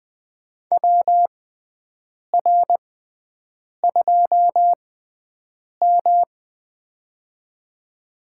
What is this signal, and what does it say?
Morse code, heard as a keyed tone, about 20 wpm, spaced as for 7 wpm, 700 Hz. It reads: WR2M
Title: USA Call Signs - 6x Character Spacing - 20wpm